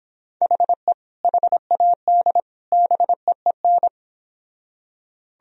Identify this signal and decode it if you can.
Morse code, heard as a keyed tone, about 26 wpm, 700 Hz. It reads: HE HAD BEEN